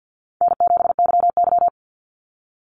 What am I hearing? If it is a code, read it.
Morse code, heard as a keyed tone, about 49 wpm, 700 Hz. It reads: N7YY
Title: Easy Call Signs Repeated 2x - 50wpm